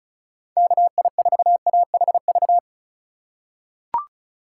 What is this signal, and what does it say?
Morse code, heard as a keyed tone, about 36 wpm, 690 Hz. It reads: KI4AHV
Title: USA Call Signs - 6 Characters - 6x Character Spacing - 35wpm